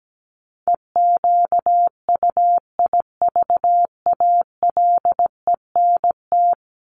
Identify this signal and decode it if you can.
Morse code, heard as a keyed tone, about 17 wpm, 700 Hz. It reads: EQUIVALENT
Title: Top 1501-1750 Words 17wpm